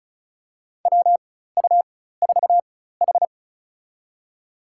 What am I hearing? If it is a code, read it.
Morse code, heard as a keyed tone, about 34 wpm, spaced as for 16 wpm, 700 Hz. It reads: WU4H
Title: USA Call Signs - 4 Characters - 4x Character Spacing - 35wpm